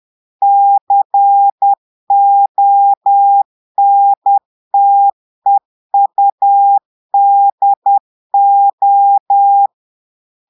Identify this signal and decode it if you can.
Morse code, heard as a keyed tone, about 10 wpm, 800 Hz. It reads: CONTEUDO
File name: portugais texte en clair 2 vitesse 10 mots minutes